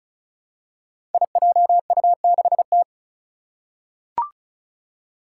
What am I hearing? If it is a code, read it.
Morse code, heard as a keyed tone, about 36 wpm, 700 Hz. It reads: IJU6T
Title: Five Challenging Letter-Number Combinations 35wpm